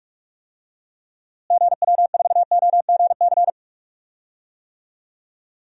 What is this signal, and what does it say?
Morse code, heard as a keyed tone, about 41 wpm, 700 Hz. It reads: GW4OGC
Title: DXCC-100 Call Signs - 6 Characters - Repeated 3x - 45wpm